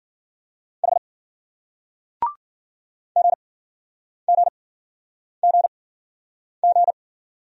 Morse code, heard as a keyed tone, about 49 wpm, spaced as for 9 wpm, 700 Hz. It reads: H GGGG
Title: Speed-Racing Single Letter-Number 40wpm